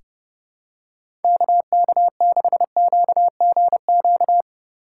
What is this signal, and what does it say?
Morse code, heard as a keyed tone, about 30 wpm, 700 Hz. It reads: KK6QGQ
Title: USA Call Signs - 6 Characters - 3x Character Spacing - 30wpm